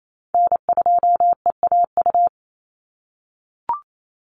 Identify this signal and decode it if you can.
Morse code, heard as a keyed tone, about 28 wpm, 700 Hz. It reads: N2EAU
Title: USA Call Signs 28wpm